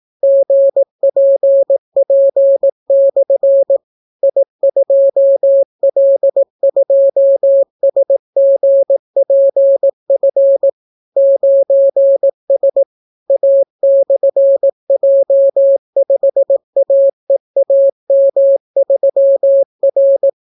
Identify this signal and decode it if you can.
Morse code, heard as a keyed tone, about 18 wpm, 550 Hz. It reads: GPP/ I2L2SGPF 9S A/J5AEAM3R